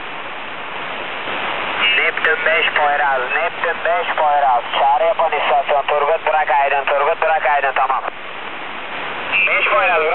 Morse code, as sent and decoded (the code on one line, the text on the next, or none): none